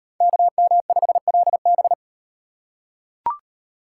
Morse code, heard as a keyed tone, about 34 wpm, 700 Hz. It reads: KM5LB